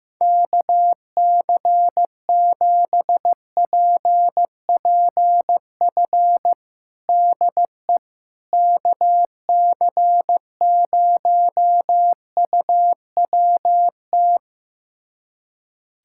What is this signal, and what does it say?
Morse code, heard as a keyed tone, about 15 wpm, 700 Hz. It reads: KC7PPF DE KC0UWT
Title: USA Callsign DE Callsign - 6 Characters - 15wpm